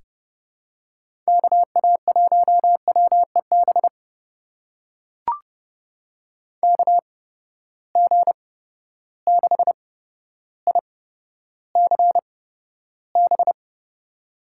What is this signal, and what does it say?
Morse code, heard as a keyed tone, about 30 wpm, spaced as for 13 wpm, 700 Hz. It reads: KA1WEB KG6ICB
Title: USA Call Signs - 6 Characters - 8x Character Spacing - 30wpm